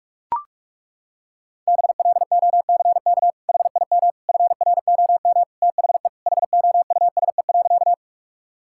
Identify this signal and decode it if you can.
Morse code, heard as a keyed tone, about 41 wpm, 700 Hz. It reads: BLOCK HIM FROM THE HOUSE.
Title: Sentences from Top 1600 Words 45wpm